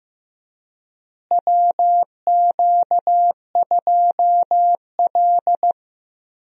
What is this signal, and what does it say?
Morse code, heard as a keyed tone, about 15 wpm, 700 Hz. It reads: WQ2L
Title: USA Call Signs - 4 Characters - Repeated 2x - 15wpm